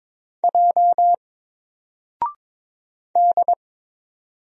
Morse code, heard as a keyed tone, about 22 wpm, 700 Hz. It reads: J D
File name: Single-Letter-Number-22wpm